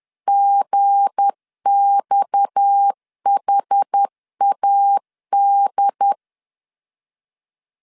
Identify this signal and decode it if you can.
Morse code, heard as a keyed tone, about 10 wpm, 800 Hz. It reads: GXHAD
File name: LEZIONI / 10 SIGNS / 10C DOT 10wpm 2940